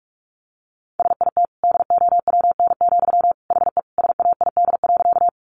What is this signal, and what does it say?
Morse code, heard as a keyed tone, about 44 wpm, 700 Hz. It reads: SIT DOWN, HE SAID.
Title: Sentences from Top 500 Words 45wpm